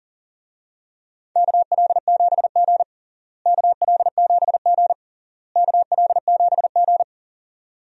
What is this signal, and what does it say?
Morse code, heard as a keyed tone, about 40 wpm, 700 Hz. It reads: KL7G KL7G KL7G
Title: DXCC-100 Call Signs - 4 Characters - Repeated 3x - 40wpm